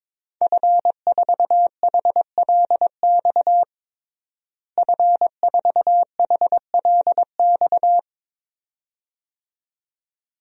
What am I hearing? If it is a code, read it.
Morse code, heard as a keyed tone, about 22 wpm, 700 Hz. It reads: F4HLX F4HLX
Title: DXCC-100 Call Signs - 5 Characters - Repeated 2x - 22wpm